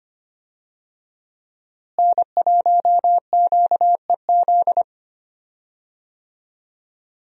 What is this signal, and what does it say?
Morse code, heard as a keyed tone, about 25 wpm, 700 Hz. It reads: N1QEZ
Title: USA Call Signs - 5 Characters - Repeated 3x - 25wpm